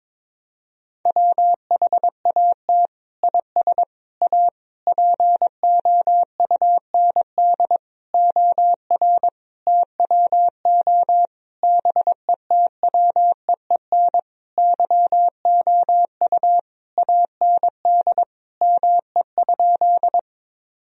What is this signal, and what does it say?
Morse code, heard as a keyed tone, about 22 wpm, 700 Hz. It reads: WHAT IS A POUND OR TWO BETWEEN YOU AND ME?
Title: Sentences from Top 1000 Words - Review Part 2 - 22wpm